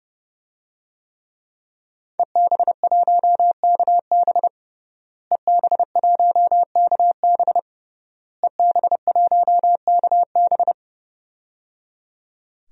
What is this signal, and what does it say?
Morse code, heard as a keyed tone, about 30 wpm, 700 Hz. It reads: EB1KB EB1KB EB1KB
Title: DXCC-100 Call Signs - 5 Characters - Repeated 3x - 30wpm